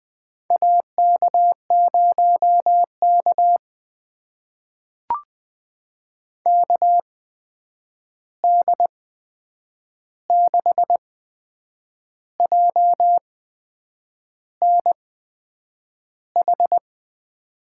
Morse code, heard as a keyed tone, about 20 wpm, spaced as for 8 wpm, 700 Hz. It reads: AK0K KD6JNH
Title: USA Call Signs - 8x Character Spacing - 20wpm